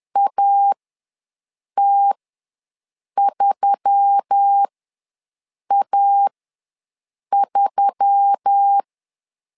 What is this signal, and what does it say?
Morse code, heard as a keyed tone, about 10 wpm, spaced as for 6 wpm, 800 Hz. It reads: AT3A3
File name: LEZIONI / 4 Q1S3 ws 10wpm 2437